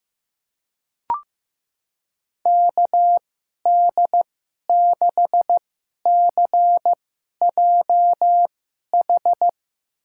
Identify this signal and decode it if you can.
Morse code, heard as a keyed tone, about 15 wpm, spaced as for 11 wpm, 700 Hz. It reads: KD6CJH
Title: USA Call Signs - 6 Characters - 2x Character Spacing - 15wpm